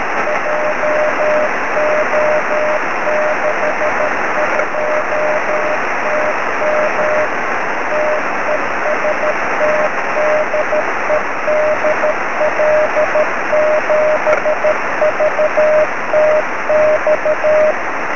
Morse code, heard as a keyed tone, about 13 wpm, 600 Hz, with noise. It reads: JO62TM TEST DE DL7VTX
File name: sm7lcb / 18-dl7vtx 23cm